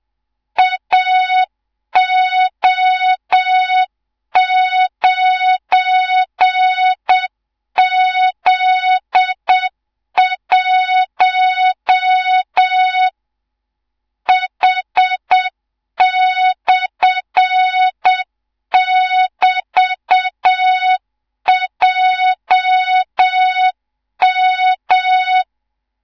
Morse code, heard as a keyed tone, about 7 wpm, 3700 Hz, with noise. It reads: AO9Z1 H/<BT>JM
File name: Morze-HA2MN / hang / 18 IRASJELEK2